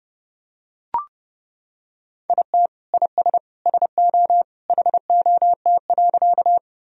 Morse code, heard as a keyed tone, about 30 wpm, 700 Hz. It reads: IT IS SO HOT.
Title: Sentences from Top 1000 Words - Review Part 3 - 30wpm